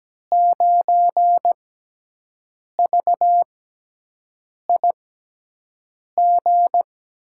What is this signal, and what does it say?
Morse code, heard as a keyed tone, about 17 wpm, spaced as for 6 wpm, 700 Hz. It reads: 9VIG
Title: USA Call Signs - 5 Characters - 6x Character Spacing - 17wpm